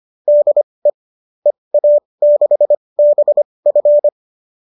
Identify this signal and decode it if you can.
Morse code, heard as a keyed tone, about 25 wpm, spaced as for 20 wpm, 570 Hz. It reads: DE EA6BF